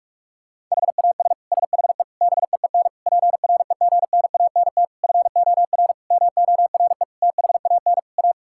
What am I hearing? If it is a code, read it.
Morse code, heard as a keyed tone, about 41 wpm, 700 Hz. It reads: HAS SHE BEEN PREGNANT FOR MORE THAN A
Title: Sentences from Top 1000 Words - Review Part 6 - 45wpm